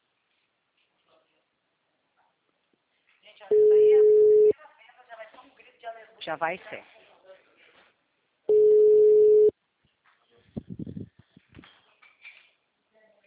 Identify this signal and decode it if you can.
Ringback tone (Europe, 425 Hz), with speech.
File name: out- -2008- - - .28